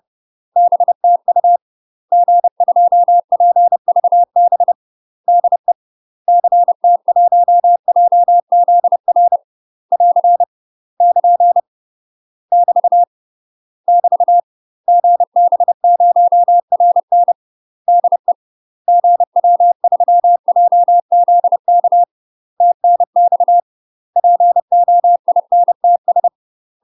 Morse code, heard as a keyed tone, about 30 wpm, 700 Hz. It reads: BTU G2PVB DE CT1JZR <AR> <KN> <BT> <BT> GB0RN DE GW3JZK TNX POINTS
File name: UK MASTER QSOs TRAINING / 30WPM / UK MASTER QSOs 0026